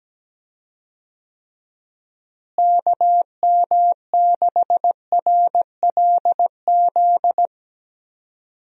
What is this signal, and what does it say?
Morse code, heard as a keyed tone, about 17 wpm, 700 Hz. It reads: KM6RLZ
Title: USA Call Signs - 6 Characters - Repeated 3x - 17wpm